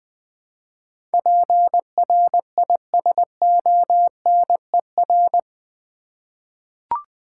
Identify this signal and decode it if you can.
Morse code, heard as a keyed tone, about 20 wpm, 700 Hz. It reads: PRISONER
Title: Eight Letter Words - 20wpm